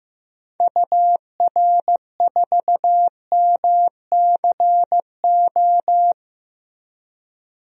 Morse code, heard as a keyed tone, about 15 wpm, 700 Hz. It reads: UR4MCO
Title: DXCC-100 Call Signs - 6 Characters - Repeated 2x - 15wpm